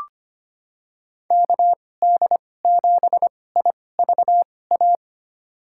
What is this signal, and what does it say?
Morse code, heard as a keyed tone, about 25 wpm, spaced as for 18 wpm, 700 Hz. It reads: KD7IVA
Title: USA Call Signs - 2x Character Spacing - 25wpm